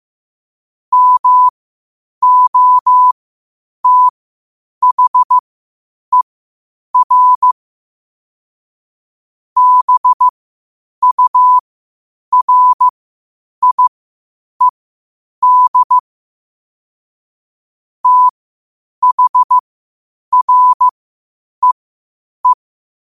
Morse code, heard as a keyed tone, about 15 wpm, spaced as for 8 wpm, 1000 Hz. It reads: MOTHER BURIED THREE
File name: qotdfiles / 8wpm